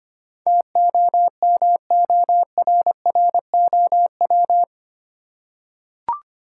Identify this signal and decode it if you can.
Morse code, heard as a keyed tone, about 25 wpm, 700 Hz. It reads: TOMORROW
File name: Eight Letter Words - 25wpm